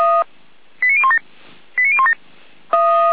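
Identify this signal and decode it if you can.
Five-tone selective call (ZVEI), heard as quick runs of five tones, tones of about 70 ms. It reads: 89017 89017
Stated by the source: Five-tone ZVEI selective call 89014, received by SDR.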